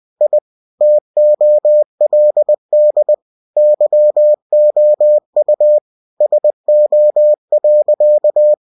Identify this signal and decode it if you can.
Morse code, heard as a keyed tone, about 20 wpm, 600 Hz. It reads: I TOLD YOU SO.